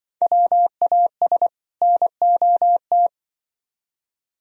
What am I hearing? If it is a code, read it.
Morse code, heard as a keyed tone, about 25 wpm, 700 Hz. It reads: WAS NOT
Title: Speed-Racing - Sets of 2 Words 20wpm